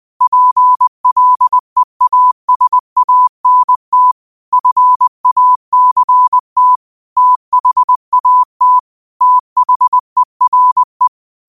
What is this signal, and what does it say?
Morse code, heard as a keyed tone, about 20 wpm, 1000 Hz. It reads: PLEASANT FACT THAT THERE
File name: qotdfiles / 20wpm